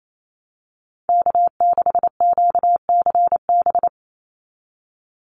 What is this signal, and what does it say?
Morse code, heard as a keyed tone, about 28 wpm, 700 Hz. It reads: K6QCB
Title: NA SOTA Chaser Callsigns 28wpm